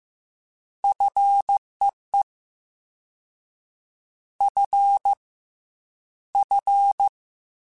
Morse code, heard as a keyed tone, about 15 wpm, 800 Hz. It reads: FEE F F
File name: Fs / Single F mp3s / Single F Prefixes 3x2 Twice @13